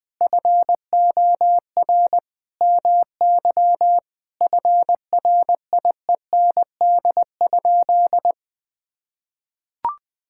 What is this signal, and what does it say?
Morse code, heard as a keyed tone, about 20 wpm, 700 Hz. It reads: FOR MY FRIEND?
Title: Sentences from Top 1000 Words - Review Part 5 - 20wpm